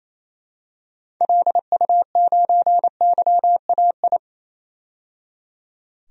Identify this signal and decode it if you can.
Morse code, heard as a keyed tone, about 28 wpm, 700 Hz. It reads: LU9YAI
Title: DXCC-100 Call Signs - 6 Characters - 28wpm